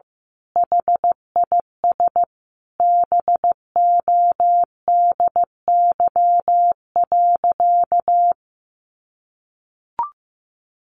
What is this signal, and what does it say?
Morse code, heard as a keyed tone, about 15 wpm, 700 Hz. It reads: HIS BODY.
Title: A Sound Of Thunder 15wpm